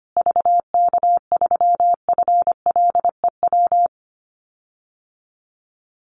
Morse code, heard as a keyed tone, about 25 wpm, 700 Hz. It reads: VK3FLEW
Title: USA Call Signs Repeated 3x 25wpm